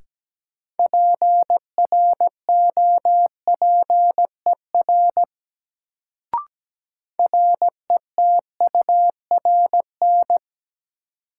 Morse code, heard as a keyed tone, about 17 wpm, 700 Hz. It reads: PROPER RETURN
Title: Sentences from Top 800 Words - New Word Drill - 17wpm